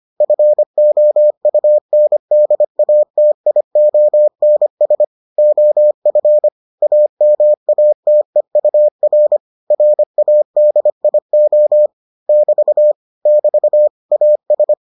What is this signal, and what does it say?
Morse code, heard as a keyed tone, about 25 wpm, 600 Hz. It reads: FOUNDATIONS OF AMATEUR RADIO <BT> <BT> AS